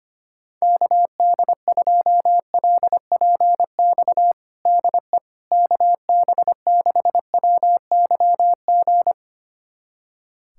Morse code, heard as a keyed tone, about 25 wpm, 700 Hz. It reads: KD2LPX DE KB6WYG
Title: USA Callsign DE Callsign - 6 Characters - 25wpm